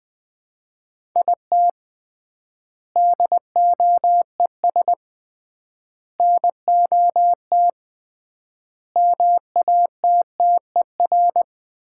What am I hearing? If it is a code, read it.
Morse code, heard as a keyed tone, about 21 wpm, 700 Hz. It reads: IT DOES NOT MATTER
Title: 3x Word-Spacing - Sets of 4 Words - 20wpm